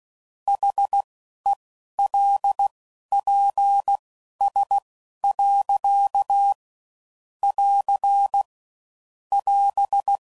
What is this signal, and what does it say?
Morse code, heard as a keyed tone, about 16 wpm, spaced as for 11 wpm, 800 Hz. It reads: HELPS. <AR> <AS>
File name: QST de WA7RSO